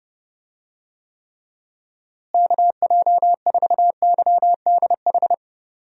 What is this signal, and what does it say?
Morse code, heard as a keyed tone, about 30 wpm, 700 Hz. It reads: KJ4YDH
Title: USA Call Signs - 6 Characters - Repeated 2x - 30wpm